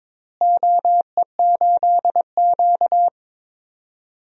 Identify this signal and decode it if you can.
Morse code, heard as a keyed tone, about 22 wpm, 700 Hz. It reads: OE8Q
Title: DXCC-100 Call Signs - 4 Characters - Repeated 2x - 22wpm